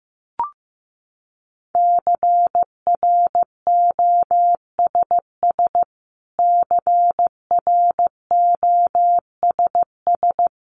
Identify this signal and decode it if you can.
Morse code, heard as a keyed tone, about 15 wpm, 700 Hz. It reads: CROSS CROSS
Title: Top 500 Words Repeated 15wpm